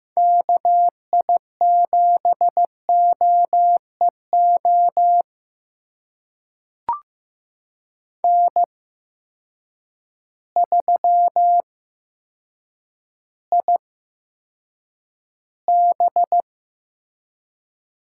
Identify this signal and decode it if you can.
Morse code, heard as a keyed tone, about 15 wpm, spaced as for 8 wpm, 700 Hz. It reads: KI7OEO N3IB
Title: USA Call Signs - 8x Character Spacing - 15wpm